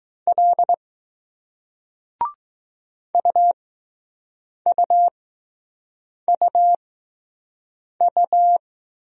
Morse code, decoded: L UUUU

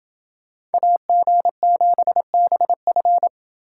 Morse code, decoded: AG7BF

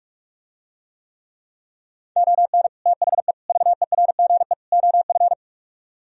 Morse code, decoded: ON THE VERGE OF

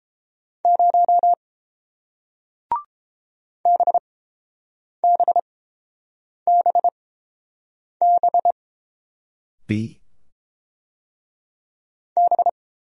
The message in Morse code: 0 B B B B B

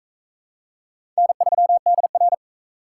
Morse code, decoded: N3DR

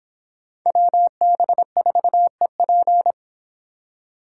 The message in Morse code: WB4EP

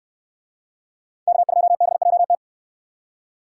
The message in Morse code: K2BJT

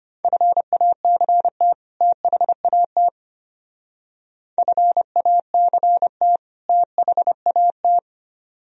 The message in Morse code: FACT THAT FACT THAT

